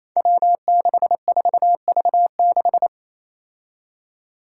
Morse code, heard as a keyed tone, about 28 wpm, 700 Hz. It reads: W64V6
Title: Five Challenging Letter-Number Combinations 28wpm